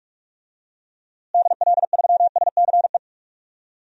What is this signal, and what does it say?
Morse code, heard as a keyed tone, about 46 wpm, 700 Hz. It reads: DL3SCE
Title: DXCC-100 Call Signs - 6 Characters - Repeated 3x - 45wpm